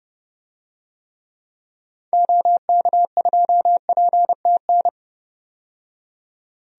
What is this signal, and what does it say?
Morse code, heard as a keyed tone, about 30 wpm, 700 Hz. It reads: OK2PTN
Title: DXCC-100 Call Signs - 6 Characters - 30wpm